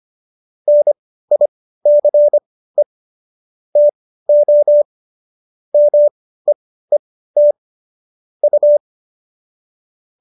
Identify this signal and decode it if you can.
Morse code, heard as a keyed tone, about 25 wpm, spaced as for 15 wpm, 590 Hz. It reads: NICE TO MEET U